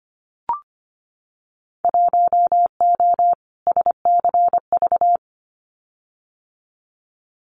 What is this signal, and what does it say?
Morse code, heard as a keyed tone, about 25 wpm, 700 Hz. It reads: 1O SCV
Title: Winter Field Day Exchanges 25wpm